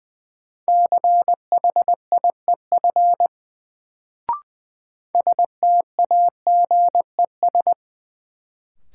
Morse code, heard as a keyed tone, about 21 wpm, 700 Hz. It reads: CHIEF STAGES
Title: Sets of 5 Words Encore - New Word Drill - 20wpm